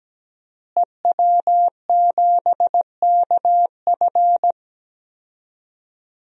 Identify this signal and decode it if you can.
Morse code, heard as a keyed tone, about 17 wpm, 700 Hz. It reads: EW7KF
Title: DXCC-100 Call Signs - 5 Characters - Repeated 2x - 17wpm